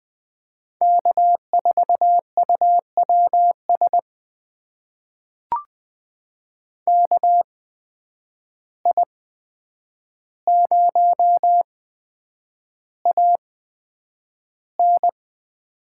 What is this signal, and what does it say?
Morse code, heard as a keyed tone, about 20 wpm, spaced as for 9 wpm, 700 Hz. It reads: K4UWS KI0AN
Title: USA Call Signs - 8x Character Spacing - 20wpm